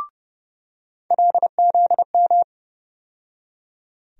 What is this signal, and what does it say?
Morse code, heard as a keyed tone, about 32 wpm, 700 Hz. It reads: LZM
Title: USA Call Sign Postfixes - 3 Letters 30wpm